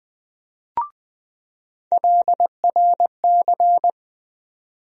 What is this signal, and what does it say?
Morse code, heard as a keyed tone, about 20 wpm, 700 Hz. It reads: LRC
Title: USA Call Sign Postfixes - 3 Letters 20wpm